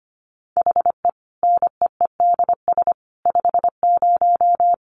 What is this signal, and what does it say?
Morse code, heard as a keyed tone, about 25 wpm, 700 Hz. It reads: HE NEEDS 50
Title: Additional Easy Phrases 25wpm